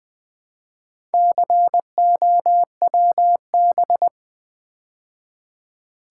Morse code, heard as a keyed tone, about 20 wpm, 690 Hz. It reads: COWB
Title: Build A Word - 2 Letters At A Time - 6-Letter Words - 20wpm